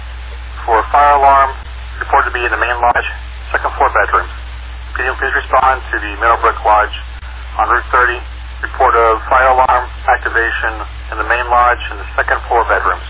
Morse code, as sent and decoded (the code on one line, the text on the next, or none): none